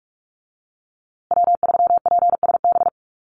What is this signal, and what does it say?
Morse code, heard as a keyed tone, about 44 wpm, 700 Hz. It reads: W3PSB
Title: USA Call Signs Repeated 2x 45wpm